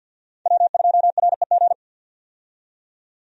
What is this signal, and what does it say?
Morse code, heard as a keyed tone, about 50 wpm, 700 Hz. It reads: W2REG
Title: USA Call Signs - 5 Characters - 50wpm